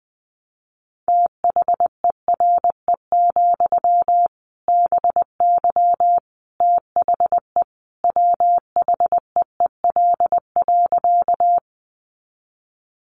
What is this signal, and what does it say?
Morse code, heard as a keyed tone, about 20 wpm, 700 Hz. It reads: THERE, BY THE WHEEL.
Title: Sentences from Top 500 Words 20wpm